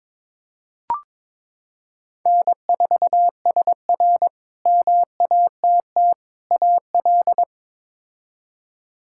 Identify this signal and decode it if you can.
Morse code, heard as a keyed tone, about 22 wpm, 700 Hz. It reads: N4SR MATT AL
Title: K1USN SST QSO Practice - 22wpm